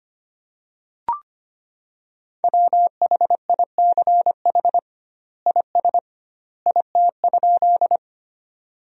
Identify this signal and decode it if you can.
Morse code, heard as a keyed tone, about 25 wpm, 700 Hz. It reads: WHICH IS IT?
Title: 2x Word-Spacing - Sentences from Top 100 Words - 25wpm